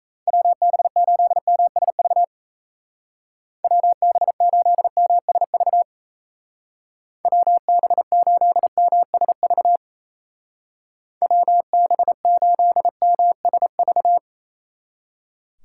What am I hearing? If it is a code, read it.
Morse code, heard as a keyed tone, about 37 wpm, 700 Hz. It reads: WB8MSV WB8MSV WB8MSV WB8MSV